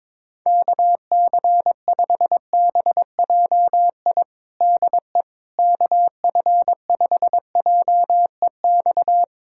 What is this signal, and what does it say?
Morse code, heard as a keyed tone, about 22 wpm, 700 Hz. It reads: KC5BJI DE KF5JEX
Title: USA Callsign DE Callsign - 6 Characters - 22wpm